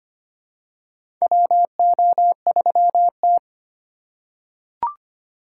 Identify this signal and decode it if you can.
Morse code, heard as a keyed tone, about 25 wpm, 700 Hz. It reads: WO3T
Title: USA Call Signs - 4 Characters - 2x Character Spacing - 25wpm